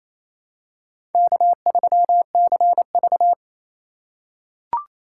Morse code, heard as a keyed tone, about 28 wpm, 700 Hz. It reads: K3CV
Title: USA Call Signs - 4 Characters - 2x Character Spacing - 28wpm